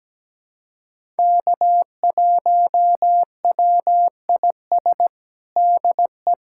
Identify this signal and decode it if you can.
Morse code, heard as a keyed tone, about 17 wpm, 700 Hz. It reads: K1WIS DE
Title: USA Callsign DE Callsign 17wpm